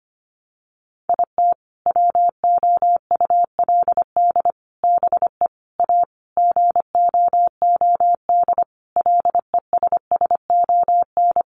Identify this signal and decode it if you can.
Morse code, heard as a keyed tone, about 25 wpm, 700 Hz. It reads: IT WOULD BE A GOOD LESSON
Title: Sentences from Top 1200 Words 25wpm